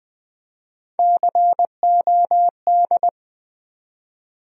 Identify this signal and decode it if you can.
Morse code, heard as a keyed tone, about 20 wpm, 700 Hz. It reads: COD
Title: USA Call Sign Postfixes - 3 Letters 20wpm